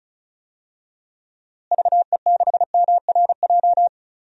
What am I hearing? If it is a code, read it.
Morse code, heard as a keyed tone, about 34 wpm, 700 Hz. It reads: VE6MRJ
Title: DXCC-100 Call Signs - 6 Characters - Repeated 2x - 35wpm